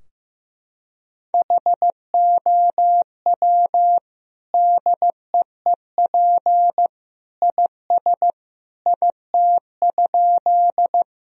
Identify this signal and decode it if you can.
Morse code, heard as a keyed tone, about 15 wpm, 700 Hz. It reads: HOW DEEP IS IT?
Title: Sentences from Top 1000 Words - Review Part 3 - 15wpm